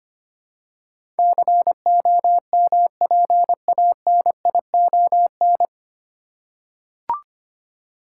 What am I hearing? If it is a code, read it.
Morse code, heard as a keyed tone, about 25 wpm, 690 Hz. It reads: COMPANION
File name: practice / 25wpm / 9 Letter Words - First 3 Letters Given - 25wpm